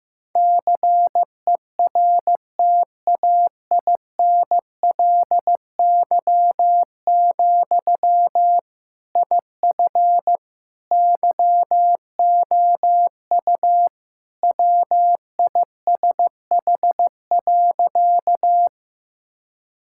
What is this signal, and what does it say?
Morse code, heard as a keyed tone, about 16 wpm, 700 Hz. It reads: CERTAINLY, IF YOU WISH.
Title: Sentences from Top 1600 Words 15wpm